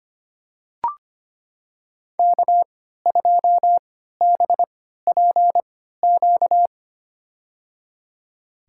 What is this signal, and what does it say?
Morse code, heard as a keyed tone, about 25 wpm, spaced as for 14 wpm, 700 Hz. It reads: K2BPQ
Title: USA Call Signs - 5 Characters - 3x Character Spacing - 25wpm